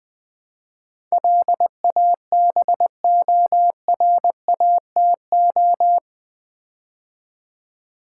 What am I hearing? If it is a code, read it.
Morse code, heard as a keyed tone, about 20 wpm, 700 Hz. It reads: LABORATO
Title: Build A Word - 4 Letters At A Time - 10-Letter Words - 20wpm